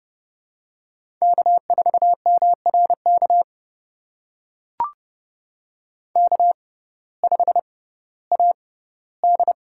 Morse code, heard as a keyed tone, about 29 wpm, spaced as for 16 wpm, 700 Hz. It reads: K4MRK K5AD
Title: USA Call Signs - 5 Characters - 6x Character Spacing - 30wpm